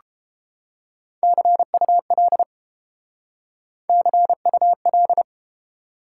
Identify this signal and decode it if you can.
Morse code, heard as a keyed tone, about 30 wpm, 700 Hz. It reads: CUL CUL